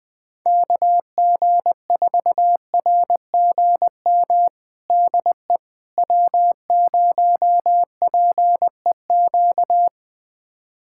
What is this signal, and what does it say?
Morse code, heard as a keyed tone, about 20 wpm, 700 Hz. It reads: KG4RGM DE W0PEQ
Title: USA Callsign DE Callsign 20wpm